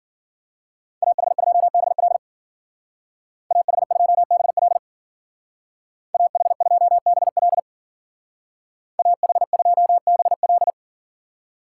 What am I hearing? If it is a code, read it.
Morse code, heard as a keyed tone, about 50 wpm, 700 Hz. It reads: AH2BL AH2BL AH2BL AH2BL
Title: Speed Racing - USA Call Signs 40wpm